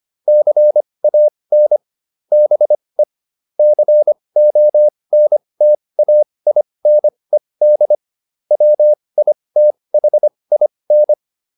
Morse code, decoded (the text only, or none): CAN BE CONTAINED WITHIN